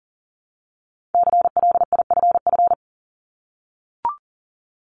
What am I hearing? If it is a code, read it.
Morse code, heard as a keyed tone, about 40 wpm, 700 Hz. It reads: CLIFF
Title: Multi-letter Phonemes Repeated 3x 40wpm